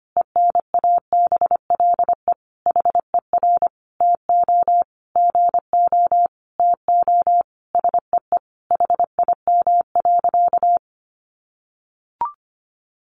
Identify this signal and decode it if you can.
Morse code, heard as a keyed tone, about 25 wpm, 700 Hz. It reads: ENABLE HER TO GO TO SEE HIM.
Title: Sentences from Top 1500 Words 25wpm